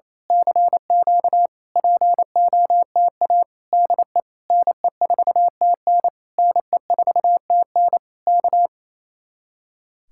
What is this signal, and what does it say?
Morse code, heard as a keyed tone, about 26 wpm, 700 Hz. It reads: CQ POTA DE NE4TN NE4TN K
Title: Speed-Racing Common QSO Phrases Encore - 28wpm